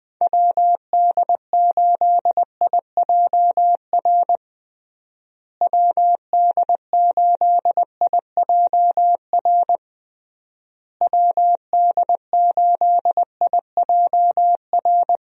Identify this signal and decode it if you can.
Morse code, heard as a keyed tone, about 20 wpm, 700 Hz. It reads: WD8IJR WD8IJR WD8IJR
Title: USA Call Signs - 6 Characters - Repeated 3x - 20wpm